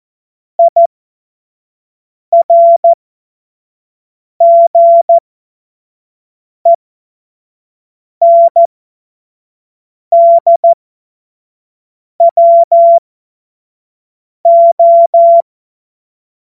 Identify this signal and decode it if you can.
Morse code, heard as a keyed tone, about 14 wpm, spaced as for 5 wpm, 680 Hz. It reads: IRGENDWO
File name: Morsekurs / Witze / Witz 02 April 2024 25 F70